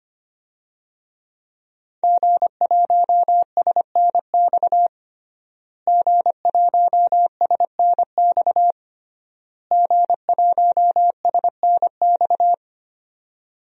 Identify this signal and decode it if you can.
Morse code, heard as a keyed tone, about 25 wpm, 700 Hz. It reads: G1SNX G1SNX G1SNX
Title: DXCC-100 Call Signs - 5 Characters - Repeated 3x - 25wpm